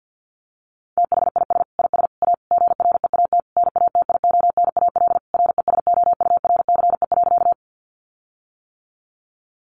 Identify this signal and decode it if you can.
Morse code, heard as a keyed tone, about 50 wpm, 700 Hz. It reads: THIS IS A GREAT NATIONAL RESOURCE.
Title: Sentences from Top 1300 Words 50wpm